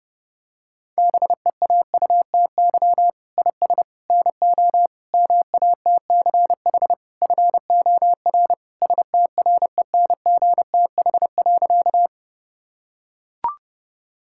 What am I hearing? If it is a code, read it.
Morse code, heard as a keyed tone, about 32 wpm, 700 Hz. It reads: BEAUTY IS NO MATCH FOR STRENGTH.
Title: Sentences from Top 1600 Words 30wpm